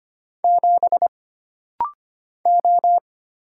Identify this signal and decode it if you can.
Morse code, heard as a keyed tone, about 25 wpm, 700 Hz. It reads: 7 O
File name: Single Character (Letter:Number:Punctuation:Prosign) - Rapid-Fire - 25wpm